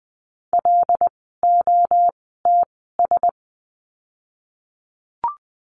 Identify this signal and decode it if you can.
Morse code, heard as a keyed tone, about 20 wpm, spaced as for 15 wpm, 700 Hz. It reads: LOTS
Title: 075 Lesson 13f - Letter L - Words 20wpm